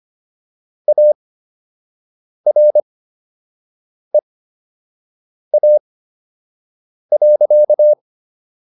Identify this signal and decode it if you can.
Morse code, heard as a keyed tone, about 25 wpm, spaced as for 6 wpm, 600 Hz. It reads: AREA.